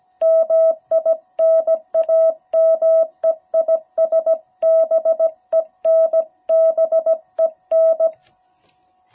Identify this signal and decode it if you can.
Morse code, heard as a keyed tone, about 17 wpm, 660 Hz. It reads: MI NAME IS BEN BEN
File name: jh8cbh / 2021cw